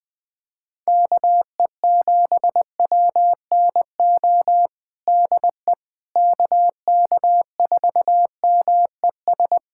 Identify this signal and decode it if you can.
Morse code, heard as a keyed tone, about 21 wpm, 700 Hz. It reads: KE7WNO DE KK4MES
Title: USA Callsign DE Callsign - 6 Characters - 20wpm